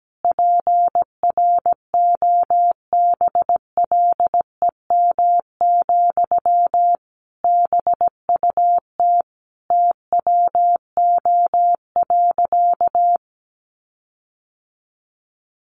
Morse code, heard as a keyed tone, about 17 wpm, 700 Hz. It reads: PROBLEM, BUT TWO.
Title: Sentences from Top 1400 Words 17wpm